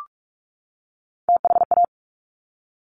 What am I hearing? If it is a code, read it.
Morse code, heard as a keyed tone, about 46 wpm, 690 Hz. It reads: THA
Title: Top Three Letter Combinations in Words 45wpm